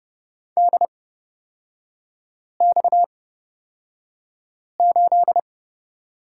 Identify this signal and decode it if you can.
Morse code, heard as a keyed tone, about 31 wpm, spaced as for 5 wpm, 690 Hz. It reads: DX8
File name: Lesson 37e - Letter x - Three Characters 30wpm